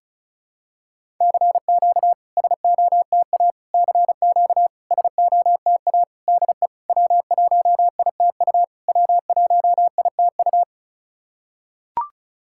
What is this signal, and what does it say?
Morse code, heard as a keyed tone, about 36 wpm, 700 Hz. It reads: CQ SOTA CQ SOTA DE W1ITU W1ITU
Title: SOTA QSO Practice 35wpm